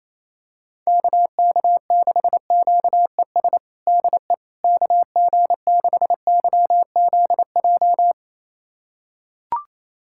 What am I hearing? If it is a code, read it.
Morse code, heard as a keyed tone, about 28 wpm, 700 Hz. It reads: KK6QES DE KG6YZJ
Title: USA Callsign DE Callsign - 6 Characters - 28wpm